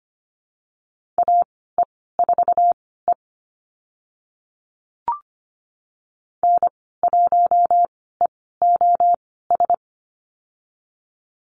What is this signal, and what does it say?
Morse code, heard as a keyed tone, about 25 wpm, spaced as for 16 wpm, 700 Hz. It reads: AE4E N1EOS
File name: Lesson 15g - Number 5 - Call Signs 25wpm